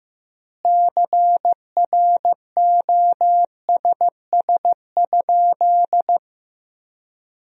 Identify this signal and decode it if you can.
Morse code, heard as a keyed tone, about 15 wpm, 700 Hz. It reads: CROSS?
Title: 3x Word-Spacing - Sentences from Top 300 Words - 15wpm